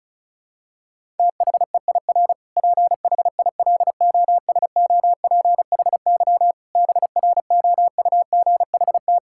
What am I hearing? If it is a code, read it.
Morse code, heard as a keyed tone, about 34 wpm, 700 Hz. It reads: THEIR PHILOSOPHY BROUGHT
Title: Sentences from Top 1000 Words - Review Part 6 - 35wpm